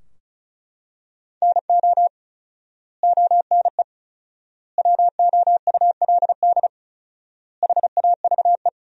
Morse code, heard as a keyed tone, about 34 wpm, 700 Hz. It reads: NO ONE WOULD HAVE
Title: 4x Word-Spacing - Sentences from Top 300 Words - 35wpm